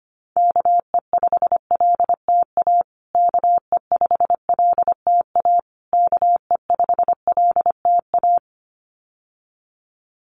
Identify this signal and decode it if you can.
Morse code, heard as a keyed tone, about 25 wpm, 700 Hz. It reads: KE5LTA KE5LTA KE5LTA
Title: USA Call Signs Repeated 3x 25wpm